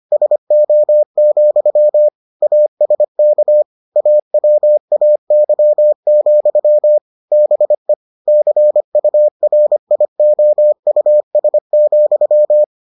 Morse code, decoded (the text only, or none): SO, ASK AWAY, BE CURIOUS,